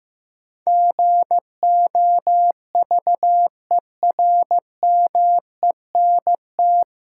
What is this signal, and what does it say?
Morse code, heard as a keyed tone, about 15 wpm, 700 Hz. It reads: GOVERMENT
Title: Sentences from Top 1300 Words - New Word Drill Alt - 15wpm